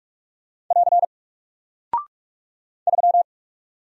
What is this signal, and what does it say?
Morse code, heard as a keyed tone, about 44 wpm, 700 Hz. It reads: <AR> 3
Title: Single Character (Letter:Number:Punctuation:Prosign) Alt 45wpm